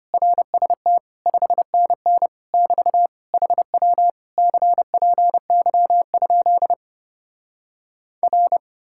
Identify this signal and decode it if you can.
Morse code, heard as a keyed tone, about 33 wpm, 700 Hz. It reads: RST 5NN <BT> HW CPY? R